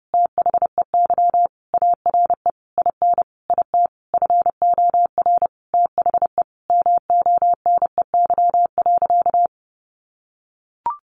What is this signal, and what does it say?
Morse code, heard as a keyed tone, about 28 wpm, 700 Hz. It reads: THEY ARE IN IT FOR THE MONEY.
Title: Sentences from Top 1300 Words 30wpm